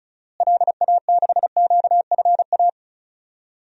Morse code, heard as a keyed tone, about 34 wpm, 700 Hz. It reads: LA6QFA
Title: DXCC-100 Call Signs - 6 Characters - Repeated 2x - 35wpm